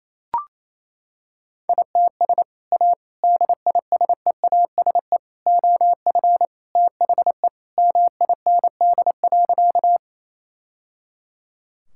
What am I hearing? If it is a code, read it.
Morse code, heard as a keyed tone, about 28 wpm, 700 Hz. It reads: ITS A DISEASE OF THE MIND.
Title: Sentences from Top 1000 Words - Review Part 6 - 28wpm